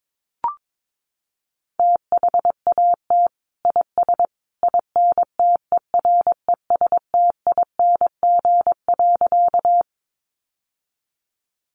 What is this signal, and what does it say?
Morse code, heard as a keyed tone, about 22 wpm, 700 Hz. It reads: THAT IS INTERESTING.
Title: Sentences from Top 1000 Words 22wpm